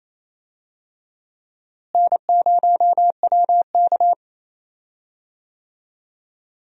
Morse code, heard as a keyed tone, about 28 wpm, 700 Hz. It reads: N0WK
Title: USA Call Signs - 4 Characters - Repeated 2x - 28wpm